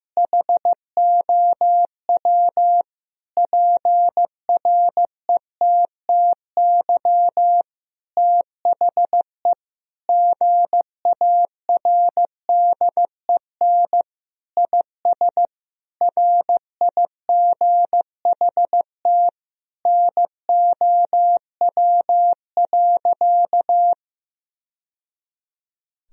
Morse code, heard as a keyed tone, about 16 wpm, 700 Hz. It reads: HOW PRETTY THE GARDEN IS RIGHT NOW.
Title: Sentences from Top 1000 Words - Review Part 4 - 15wpm